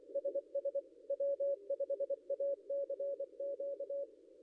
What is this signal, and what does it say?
Morse code, heard as a keyed tone, about 25 wpm, 560 Hz. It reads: SS W5ACQ
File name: contests / ss-cw-k5tr-op-n5rz -0800